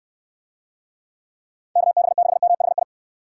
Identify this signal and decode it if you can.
Morse code, heard as a keyed tone, about 69 wpm, 700 Hz. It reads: KB6MHI